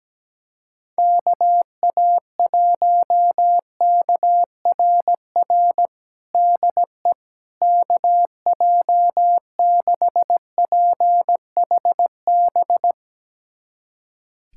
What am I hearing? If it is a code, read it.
Morse code, heard as a keyed tone, about 17 wpm, 700 Hz. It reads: KA1KRR DE KJ6PHB
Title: USA Callsign DE Callsign - 6 Characters - 17wpm